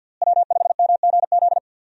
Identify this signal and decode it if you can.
Morse code, heard as a keyed tone, about 50 wpm, 700 Hz. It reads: W5MGZ
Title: USA Call Signs - 5 Characters - Repeated 2x - 50wpm